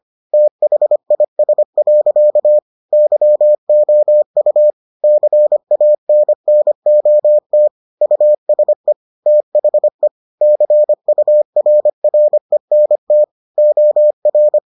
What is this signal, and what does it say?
Morse code, heard as a keyed tone, about 25 wpm, 600 Hz. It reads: THIS. YOU CANNOT USE THE CURRENT OR